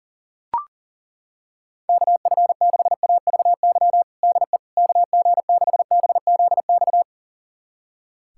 Morse code, heard as a keyed tone, about 40 wpm, 700 Hz. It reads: KF6AVY DE KG6BZX
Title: USA Callsign DE Callsign - 6 Characters - 40wpm